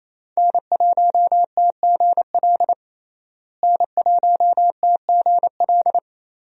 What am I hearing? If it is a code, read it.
Morse code, heard as a keyed tone, about 27 wpm, 700 Hz. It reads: N1TGL N1TGL
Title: USA Call Signs - 5 Characters - Repeated 2x - 28wpm